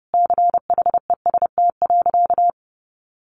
Morse code, heard as a keyed tone, about 30 wpm, 700 Hz. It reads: CHEST.
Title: A Sound Of Thunder 30wpm